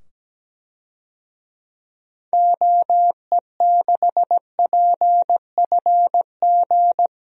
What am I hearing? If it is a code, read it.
Morse code, heard as a keyed tone, about 17 wpm, 700 Hz. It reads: OE6PFG
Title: DXCC-100 Call Signs - 6 Characters - 17wpm